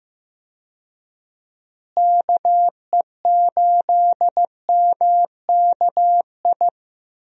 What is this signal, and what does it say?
Morse code, heard as a keyed tone, about 15 wpm, 700 Hz. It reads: KE8MKI
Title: USA Call Signs - 6 Characters - Repeated 3x - 15wpm